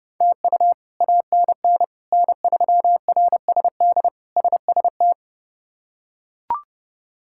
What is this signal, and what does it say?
Morse code, heard as a keyed tone, about 30 wpm, 700 Hz. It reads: TU ANN N3RSD SST
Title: K1USN SST QSO Practice - 30wpm